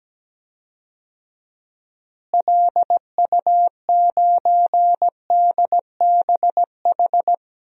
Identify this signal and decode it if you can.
Morse code, heard as a keyed tone, about 17 wpm, 700 Hz. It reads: LU9DBH
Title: DXCC-100 Call Signs - 6 Characters - Repeated 2x - 17wpm